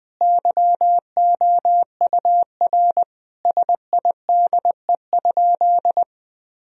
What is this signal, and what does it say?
Morse code, heard as a keyed tone, about 21 wpm, 700 Hz. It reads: YOUR SIDE?
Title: Sentences from Top 1000 Words - Review Part 1 - 20wpm